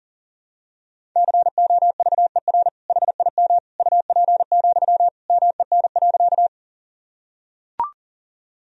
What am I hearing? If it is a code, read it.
Morse code, heard as a keyed tone, about 37 wpm, 700 Hz. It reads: COVER HIM UP, MEN.
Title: Sentences from Top 1000 Words - Review Part 1 - 40wpm